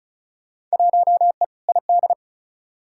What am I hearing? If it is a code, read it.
Morse code, heard as a keyed tone, about 34 wpm, 700 Hz. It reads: 1E ID
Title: Field Day Exchanges - 35wpm